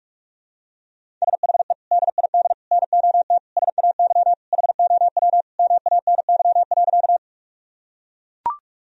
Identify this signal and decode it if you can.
Morse code, heard as a keyed tone, about 43 wpm, 700 Hz. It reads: SHE DID NOT SAY HOW MANY.
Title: Sentences from Top 1000 Words - Review Part 1 - 45wpm